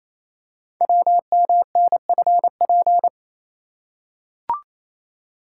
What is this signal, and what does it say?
Morse code, heard as a keyed tone, about 28 wpm, 700 Hz. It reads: WMNFP
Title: Code Groups - 5 Letters - 28wpm